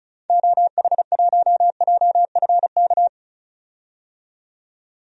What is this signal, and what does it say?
Morse code, heard as a keyed tone, about 34 wpm, 700 Hz. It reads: OH1JFK
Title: DXCC-100 Call Signs - 6 Characters - 35wpm